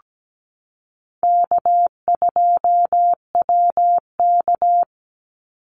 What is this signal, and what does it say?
Morse code, heard as a keyed tone, about 17 wpm, 700 Hz. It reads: K2WK
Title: Easy Call Signs Repeated 3x - 17wpm